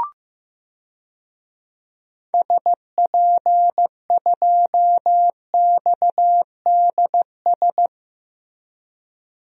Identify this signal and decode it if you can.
Morse code, heard as a keyed tone, about 16 wpm, 700 Hz. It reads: SP2XDS
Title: DXCC-100 Call Signs - 6 Characters - Repeated 2x - 15wpm